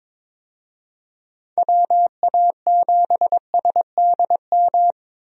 Morse code, heard as a keyed tone, about 22 wpm, 700 Hz. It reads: WA7SDM
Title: USA Call Signs - 6 Characters - 22wpm